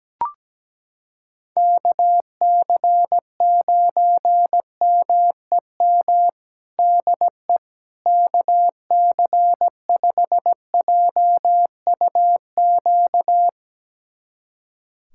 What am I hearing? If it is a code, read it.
Morse code, heard as a keyed tone, about 17 wpm, 700 Hz. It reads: KC9MEM DE KC5JUQ
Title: USA Callsign DE Callsign - 6 Characters - 17wpm